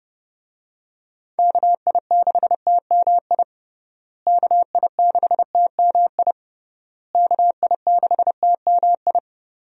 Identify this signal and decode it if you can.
Morse code, heard as a keyed tone, about 32 wpm, 700 Hz. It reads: KI6TMI KI6TMI KI6TMI
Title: USA Call Signs - 6 Characters - Repeated 3x - 30wpm